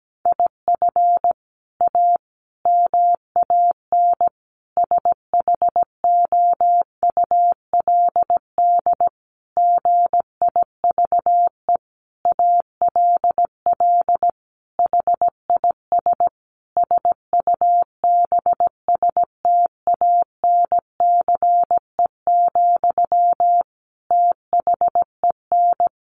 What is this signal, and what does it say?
Morse code, heard as a keyed tone, about 17 wpm, 700 Hz. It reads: IF A MAN SHOULD GIVE ALL HIS SUBSTANCE, THEN
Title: Sentences from Top 1500 Words 17wpm